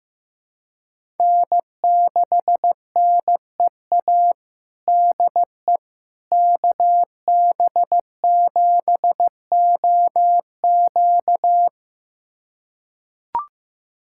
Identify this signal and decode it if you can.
Morse code, heard as a keyed tone, about 16 wpm, 700 Hz. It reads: N6NEA DE KB7OQ
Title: USA Callsign DE Callsign - 5 Characters - 15wpm